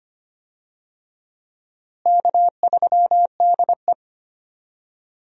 Morse code, heard as a keyed tone, about 25 wpm, 700 Hz. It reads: K3DE
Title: USA Call Signs - 4 Characters - Repeated 2x - 25wpm